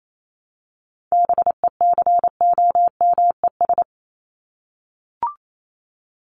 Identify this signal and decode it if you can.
Morse code, heard as a keyed tone, about 28 wpm, 700 Hz. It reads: BECOMES
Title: ES:S Suffix - Top 100 Words with Base Words 28wpm